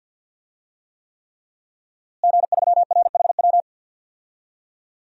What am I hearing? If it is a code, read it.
Morse code, heard as a keyed tone, about 50 wpm, 700 Hz. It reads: G3RHW
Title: DXCC-100 Call Signs - 5 Characters - Repeated 3x - 50wpm